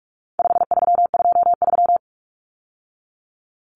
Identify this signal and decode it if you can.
Morse code, heard as a keyed tone, about 44 wpm, 700 Hz. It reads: 5323
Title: Numbers - 4-Digit 45wpm